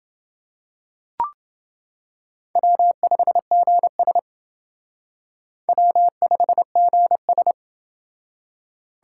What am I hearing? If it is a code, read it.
Morse code, heard as a keyed tone, about 29 wpm, 700 Hz. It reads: W5GS W5GS